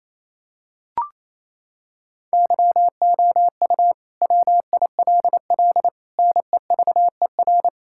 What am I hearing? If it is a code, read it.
Morse code, heard as a keyed tone, about 27 wpm, 700 Hz. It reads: YOU WILL NEVER
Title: Sentences from Top 1600 Words 28wpm